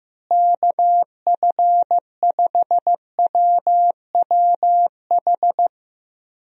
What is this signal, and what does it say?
Morse code, heard as a keyed tone, about 15 wpm, 700 Hz. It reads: KF5WWH
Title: USA Call Signs - 6 Characters - Repeated 2x - 15wpm